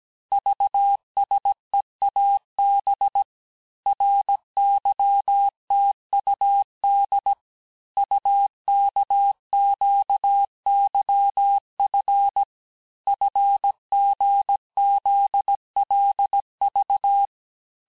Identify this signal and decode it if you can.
Morse code, heard as a keyed tone, about 17 wpm, 800 Hz. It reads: VSEAB RYTUD UKQYF FGZLV
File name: corrige-cours-1581-texte-chiffre-vitesse-16-mpm